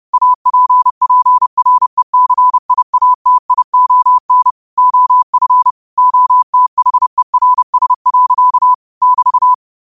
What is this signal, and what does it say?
Morse code, heard as a keyed tone, about 30 wpm, 1000 Hz. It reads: APPRECIATION OF OTHERS. <BT>